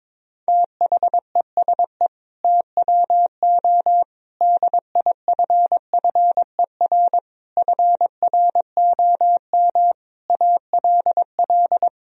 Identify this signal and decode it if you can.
Morse code, heard as a keyed tone, about 22 wpm, 700 Hz. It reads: THESE TWO DIFFER FROM ALL
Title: Sentences from Top 1000 Words - Review Part 1 - 22wpm